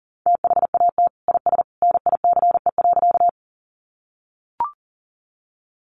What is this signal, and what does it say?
Morse code, heard as a keyed tone, about 37 wpm, 700 Hz. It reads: THAT IS NICE.
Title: Sentences from Top 1100 Words 40wpm